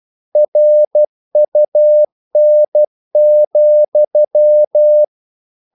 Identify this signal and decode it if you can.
Morse code, heard as a keyed tone, about 12 wpm, 600 Hz. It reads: RUN,
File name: CW / Practice / If12